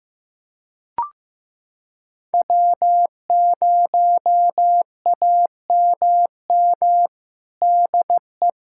Morse code, heard as a keyed tone, about 15 wpm, 700 Hz. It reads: W0AMM DE
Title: USA Callsign DE Callsign 15wpm